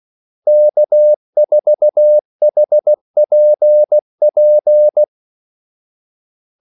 Morse code, heard as a keyed tone, about 16 wpm, 600 Hz. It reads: K4HPP
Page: Kaldesignaler 16 wpm